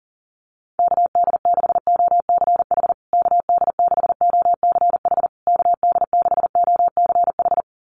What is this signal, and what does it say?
Morse code, heard as a keyed tone, about 40 wpm, 700 Hz. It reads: KD6OCH KD6OCH KD6OCH
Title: USA Call Signs Repeated 3x 40wpm